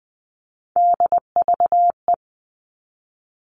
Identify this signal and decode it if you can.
Morse code, heard as a keyed tone, about 20 wpm, 700 Hz. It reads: DVE
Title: All Cut Numbers - 3-Digit 20wpm